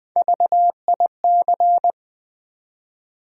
Morse code, heard as a keyed tone, about 20 wpm, 700 Hz. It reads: VIC
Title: Build A Word - 1 Letter At A Time - 4-Letter Words - 20wpm